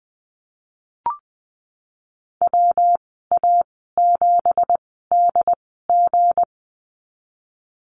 Morse code, heard as a keyed tone, about 20 wpm, spaced as for 15 wpm, 700 Hz. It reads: WA7DG
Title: 173 Lesson 28g - Number 9 - Call Signs 20wpm